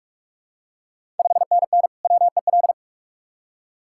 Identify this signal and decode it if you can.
Morse code, heard as a keyed tone, about 43 wpm, 700 Hz. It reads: 5NN WEL
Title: Ontario QSO Party Exchange 45wpm